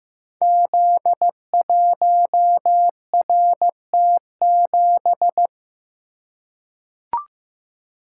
Morse code, heard as a keyed tone, about 15 wpm, 700 Hz. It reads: Z1RT7
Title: Code Groups - 5 Letter:Numbers - 15wpm